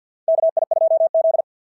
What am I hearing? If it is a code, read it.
Morse code, heard as a keyed tone, about 50 wpm, 660 Hz. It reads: KI2Z
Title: SOTA QSO Practice 50wpm